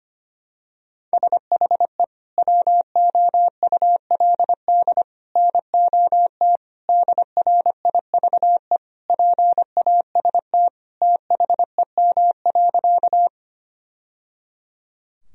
Morse code, heard as a keyed tone, about 25 wpm, 700 Hz. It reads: SHE WOULD NOT DRIVE PAST THEM.
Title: Sentences from Top 1000 Words - Review Part 3 - 25wpm